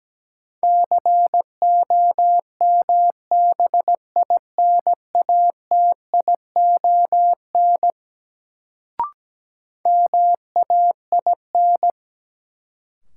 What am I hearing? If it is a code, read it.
Morse code, heard as a keyed tone, about 17 wpm, 700 Hz. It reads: COMBINATION MAIN
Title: Sentences from Top 1300 Words - New Word Drill Alt - 17wpm